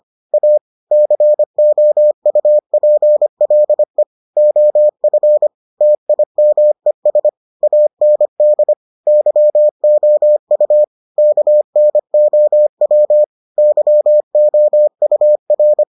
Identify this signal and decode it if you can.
Morse code, heard as a keyed tone, about 25 wpm, 600 Hz. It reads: A COUPLE OF TIMES AND YOU KNOW YOUR